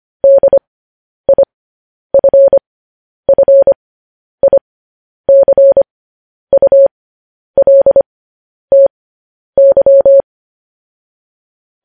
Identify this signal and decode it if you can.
Morse code, heard as a keyed tone, about 25 wpm, spaced as for 10 wpm, 550 Hz. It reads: DIFFICULTY
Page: news headlines in morse code at 10 wpm